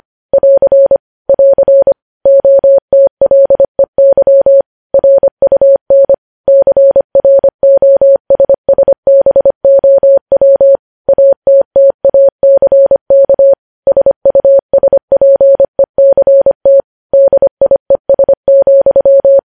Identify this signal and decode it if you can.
Morse code, heard as a keyed tone, about 25 wpm, 550 Hz. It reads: <AR> <AR> OTLEY RUN CROSSBOW ATTACK SUSPECT DIES,
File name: podcast / News headlines-1x-25x25